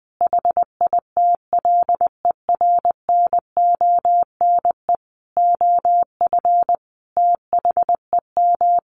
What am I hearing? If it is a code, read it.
Morse code, heard as a keyed tone, about 20 wpm, 700 Hz. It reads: HITLER NONE OF THEM
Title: A Sound Of Thunder 20wpm